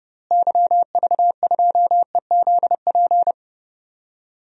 Morse code, heard as a keyed tone, about 30 wpm, 700 Hz. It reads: YV2EZP
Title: DXCC-100 Call Signs - 6 Characters - Repeated 3x - 30wpm